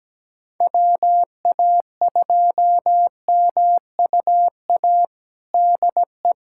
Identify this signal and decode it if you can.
Morse code, heard as a keyed tone, about 17 wpm, 700 Hz. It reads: WA2MUA DE
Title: USA Callsign DE Callsign 17wpm